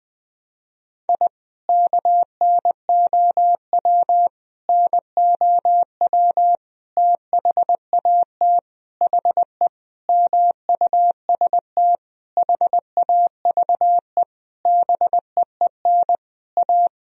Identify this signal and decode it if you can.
Morse code, heard as a keyed tone, about 20 wpm, 700 Hz. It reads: I KNOW NOW THAT HE MUST HAVE BEEN A
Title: Sentences from Top 1000 Words - Review Part 1 - 20wpm